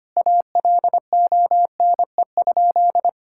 Morse code, heard as a keyed tone, about 25 wpm, 700 Hz. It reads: ALONE?
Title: Sentences from Top 1600 Words 25wpm